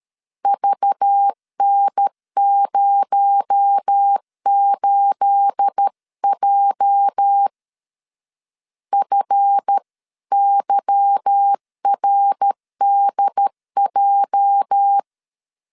Morse code, heard as a keyed tone, about 13 wpm, 800 Hz. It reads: VN08J FYRDJ
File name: CW / DET / 1 9 1500C S B 12wpm 3819